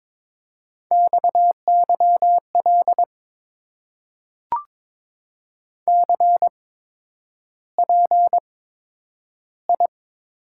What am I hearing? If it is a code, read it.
Morse code, heard as a keyed tone, about 22 wpm, spaced as for 9 wpm, 700 Hz. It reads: XYL CPI
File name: Top 90 QSO Elements - 8x Character Spacing - 22wpm